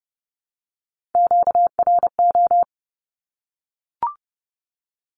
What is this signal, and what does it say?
Morse code, heard as a keyed tone, about 30 wpm, 700 Hz. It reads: QRO
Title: Q-Codes 30wpm